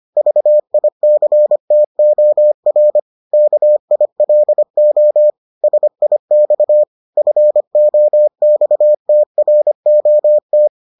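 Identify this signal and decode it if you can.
Morse code, heard as a keyed tone, about 25 wpm, 600 Hz. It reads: VICTOR KILO SIX FOXTROT
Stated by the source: Morse edition of the Foundations of Amateur Radio episodes.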